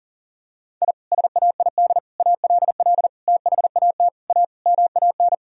Morse code, heard as a keyed tone, about 40 wpm, 700 Hz. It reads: I SAID ALL THAT A MAN
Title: Sentences from Top 1000 Words - Review Part 2 - 40wpm